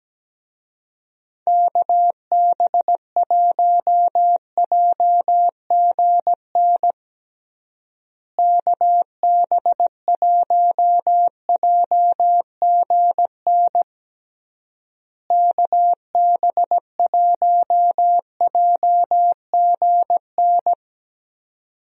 Morse code, heard as a keyed tone, about 17 wpm, 700 Hz. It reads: KB1JGN KB1JGN KB1JGN
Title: USA Call Signs - 6 Characters - Repeated 3x - 17wpm